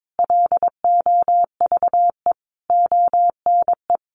Morse code, heard as a keyed tone, about 22 wpm, 700 Hz. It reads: LOVE ONE
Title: Sentences from Top 600 Words 22wpm